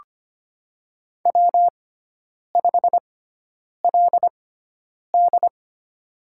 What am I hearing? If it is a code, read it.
Morse code, heard as a keyed tone, about 25 wpm, spaced as for 9 wpm, 700 Hz. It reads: W5LD
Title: USA Call Signs - 4 Characters - 6x Character Spacing - 25wpm